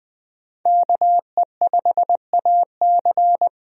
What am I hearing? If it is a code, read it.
Morse code, heard as a keyed tone, about 21 wpm, 700 Hz. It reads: KE5AC
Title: USA Call Signs - 5 Characters - Repeated 2x - 20wpm